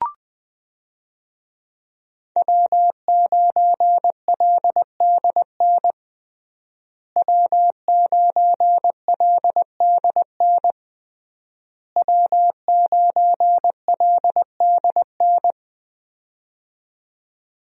Morse code, heard as a keyed tone, about 20 wpm, 700 Hz. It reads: W9LDN W9LDN W9LDN
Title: USA Call Signs - 5 Characters - Repeated 3x - 20wpm